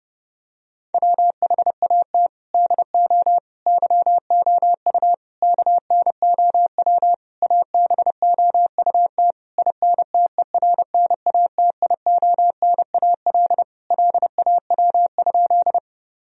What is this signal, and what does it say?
Morse code, heard as a keyed tone, about 30 wpm, 700 Hz. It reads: WHAT DO YOU KNOW ABOUT INTERNATIONAL LAW?
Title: Sentences from Top 1000 Words - Review Part 4 - 30wpm